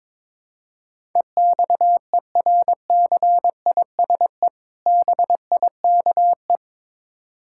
Morse code, heard as a keyed tone, about 22 wpm, 700 Hz. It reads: EXERCISE BIKE
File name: Easy Words - Speed-Racing with Hints - 22wpm